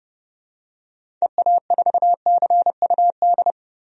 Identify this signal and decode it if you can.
Morse code, heard as a keyed tone, about 32 wpm, 700 Hz. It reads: EA4CUD
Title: DXCC-100 Call Signs - 6 Characters - Repeated 2x - 30wpm